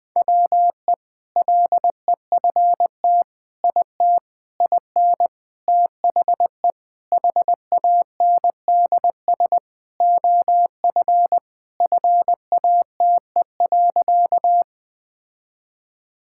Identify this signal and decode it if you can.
Morse code, heard as a keyed tone, about 20 wpm, 690 Hz. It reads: WE LEFT IT IN THE HANDS OF FATE.
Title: Sentences with 4-Word Phrases - Spoken Beginning - 20wpm